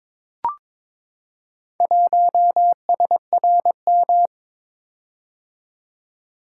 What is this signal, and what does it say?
Morse code, heard as a keyed tone, about 23 wpm, 700 Hz. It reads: 1SRM
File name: practice / 22wpm / Four Letter-Number Combinations - 22wpm